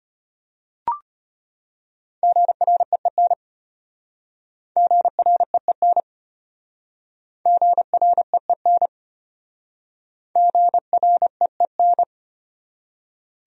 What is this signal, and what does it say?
Morse code, heard as a keyed tone, about 33 wpm, 700 Hz. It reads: GREEN GREEN GREEN GREEN